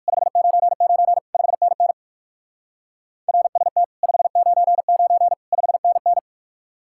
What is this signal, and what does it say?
Morse code, heard as a keyed tone, about 56 wpm, 700 Hz. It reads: 599 5NN RST 599 5NN
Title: Speed-Racing Common QSO Phrases - 45wpm